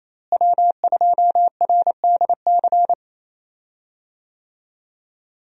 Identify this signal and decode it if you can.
Morse code, heard as a keyed tone, about 28 wpm, 700 Hz. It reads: W2RDC
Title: USA Call Signs - 5 Characters - Repeated 3x - 28wpm